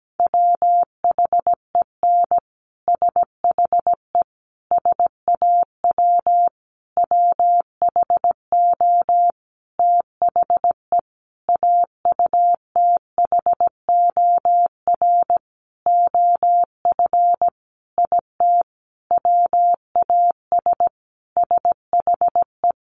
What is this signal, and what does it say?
Morse code, heard as a keyed tone, about 17 wpm, 700 Hz. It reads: WHEN SHE SAW WHO THE AUTHOR OF IT WAS SHE
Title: Sentences from Top 1200 Words 17wpm